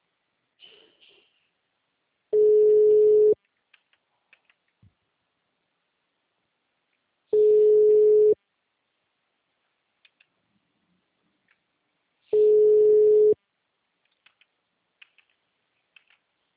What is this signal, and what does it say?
Ringback tone (Europe, 425 Hz).